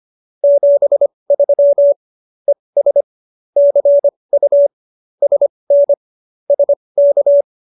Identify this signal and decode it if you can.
Morse code, heard as a keyed tone, about 25 wpm, spaced as for 20 wpm, 570 Hz. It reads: 73 ES CU SN SK